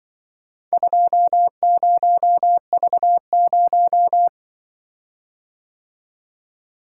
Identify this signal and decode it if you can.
Morse code, heard as a keyed tone, about 23 wpm, 700 Hz. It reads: 20V0
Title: Speed-Racing - Four Letter-Number Combinations - 20wpm